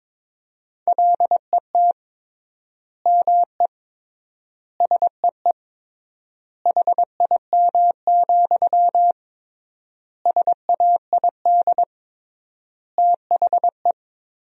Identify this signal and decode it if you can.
Morse code, heard as a keyed tone, about 21 wpm, 700 Hz. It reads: LET ME SEE HIM, SAID THE
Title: 3x Word-Spacing - Sentences from Top 400 Words - 22wpm